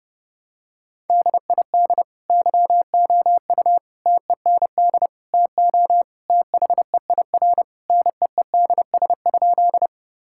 Morse code, heard as a keyed tone, about 32 wpm, 700 Hz. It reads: DID YOU TEND TO THEIR NEEDS?
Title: Sentences from Top 1000 Words - Review Part 5 - 30wpm